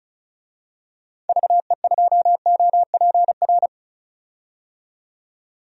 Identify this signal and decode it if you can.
Morse code, heard as a keyed tone, about 34 wpm, 700 Hz. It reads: VE2OPR
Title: DXCC-100 Call Signs - 6 Characters - 35wpm